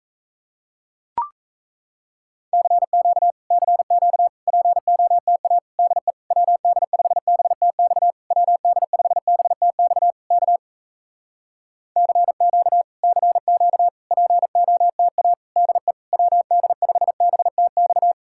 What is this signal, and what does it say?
Morse code, heard as a keyed tone, about 37 wpm, 700 Hz. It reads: CQ CQ POTA DE WD5BTX WD5BTX K CQ CQ POTA DE WD5BTX